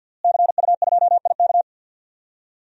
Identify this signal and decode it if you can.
Morse code, heard as a keyed tone, about 46 wpm, 700 Hz. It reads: CU2IK
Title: DXCC-100 Call Signs - 5 Characters - Repeated 2x - 50wpm